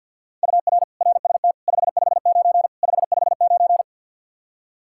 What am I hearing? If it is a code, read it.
Morse code, heard as a keyed tone, about 52 wpm, 700 Hz. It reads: UR RST 559 559
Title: Practice QSO - Exchange 1 - Beginning - 50wpm